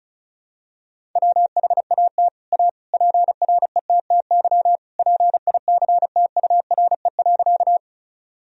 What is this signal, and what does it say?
Morse code, heard as a keyed tone, about 36 wpm, 700 Hz. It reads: WHAT A PRETTY PICTURE.
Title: Sentences from Top 1000 Words - Review Part 1 - 35wpm